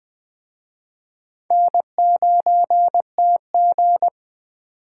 Morse code, heard as a keyed tone, about 24 wpm, 700 Hz. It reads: N9TG
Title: Speed Racing - USA 1x2 Call Signs 20wpm